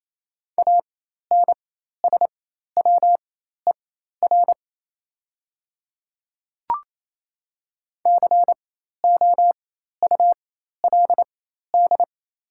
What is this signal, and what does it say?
Morse code, heard as a keyed tone, about 28 wpm, spaced as for 13 wpm, 700 Hz. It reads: ANSWER COULD
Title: 4x Character Spacing - Top 500 Words - 28wpm